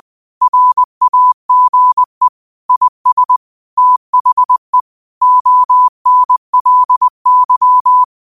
Morse code, heard as a keyed tone, about 20 wpm, 1000 Hz. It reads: RAGE IS THE ONLY
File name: qotdfiles / 20wpm